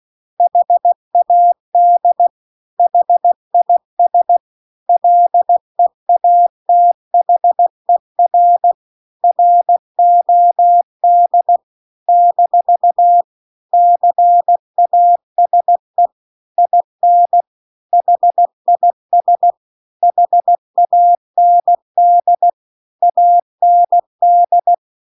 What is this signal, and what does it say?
Morse code, heard as a keyed tone, about 16 wpm, 700 Hz. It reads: HAD HIS LEATHER ROD - CASE IN HIS HAND AND